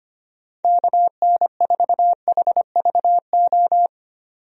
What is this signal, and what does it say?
Morse code, heard as a keyed tone, about 25 wpm, 700 Hz. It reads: KN4HVO
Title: USA Call Signs - 6 Characters - 25wpm